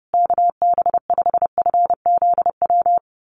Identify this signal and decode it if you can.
Morse code, heard as a keyed tone, about 30 wpm, 700 Hz. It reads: KB5FZW
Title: USA Call Signs Repeated 3x 30wpm